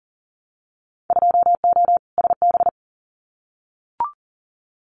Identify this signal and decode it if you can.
Morse code, heard as a keyed tone, about 40 wpm, 700 Hz. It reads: 2O SB
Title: Winter Field Day Exchanges 40wpm